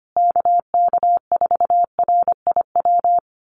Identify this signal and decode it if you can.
Morse code, heard as a keyed tone, about 25 wpm, 700 Hz. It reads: KK4RIW
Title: USA Call Signs Repeated 3x 25wpm